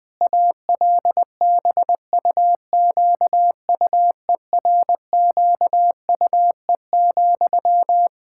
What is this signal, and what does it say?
Morse code, heard as a keyed tone, about 20 wpm, 700 Hz. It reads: ALBUQUERQUE,
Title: Practice QSO - Exchange 1 - Beginning - 20wpm